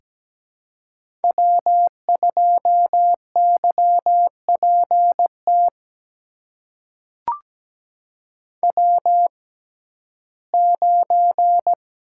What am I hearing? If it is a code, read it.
Morse code, heard as a keyed tone, about 17 wpm, spaced as for 12 wpm, 700 Hz. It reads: W2YPT W9
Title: USA Call Signs - 6x Character Spacing - 17wpm